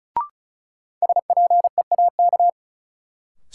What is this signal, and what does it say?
Morse code, heard as a keyed tone, about 36 wpm, 700 Hz. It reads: SPEAK
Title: Sentences from Top 1000 Words - Review Part 2 - New Word Drill - 35wpm